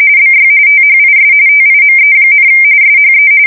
RTTY, about 45.45 baud. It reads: BY THIS TIME, SHE
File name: fsk170 / USS-Massachusetts